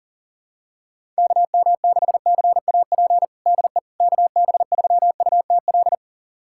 Morse code, heard as a keyed tone, about 40 wpm, 700 Hz. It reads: KM6CAP DE KB3UTL
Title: USA Callsign DE Callsign 40wpm